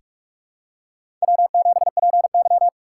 Morse code, heard as a keyed tone, about 46 wpm, 700 Hz. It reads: W7PY
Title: USA Call Signs - 4 Characters - 4x Character Spacing - 45wpm